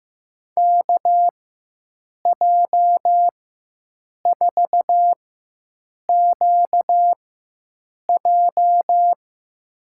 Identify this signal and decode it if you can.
Morse code, heard as a keyed tone, about 15 wpm, spaced as for 7 wpm, 700 Hz. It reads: KJ4QJ
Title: USA Call Signs - 5 Characters - 4x Character Spacing - 15wpm